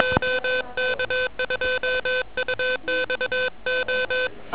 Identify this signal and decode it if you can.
Morse code, heard as a keyed tone, about 22 wpm, 510 Hz, with noise. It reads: OK2UXO
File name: OK QRP 2007 / REC31